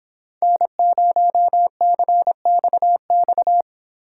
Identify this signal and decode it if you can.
Morse code, heard as a keyed tone, about 26 wpm, 700 Hz. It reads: N0CXX